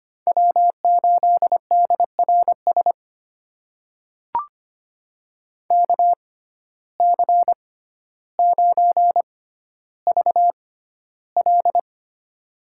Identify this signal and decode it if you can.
Morse code, heard as a keyed tone, about 25 wpm, spaced as for 13 wpm, 700 Hz. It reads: W8DRS KC9VL
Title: USA Call Signs - 5 Characters - 6x Character Spacing - 25wpm